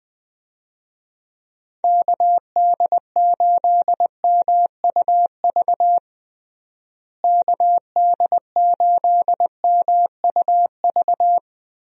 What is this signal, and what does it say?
Morse code, heard as a keyed tone, about 20 wpm, 700 Hz. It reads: KD8MUV KD8MUV
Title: USA Call Signs - 6 Characters - Repeated 3x - 20wpm